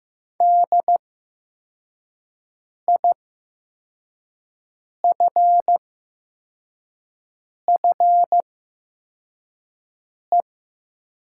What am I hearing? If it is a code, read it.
Morse code, heard as a keyed tone, about 15 wpm, spaced as for 4 wpm, 700 Hz. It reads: DIFFE
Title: 8x Character Spacing - Top 300 Words - 15wpm